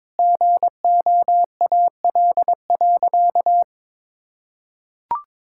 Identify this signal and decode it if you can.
Morse code, heard as a keyed tone, about 22 wpm, 700 Hz. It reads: GOAL.
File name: Sentences with 7-Word Phrases - 22wpm